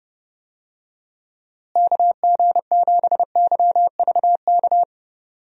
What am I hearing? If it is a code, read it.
Morse code, heard as a keyed tone, about 30 wpm, 700 Hz. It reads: KG7YVK
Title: USA Call Signs - 6 Characters - Repeated 2x - 30wpm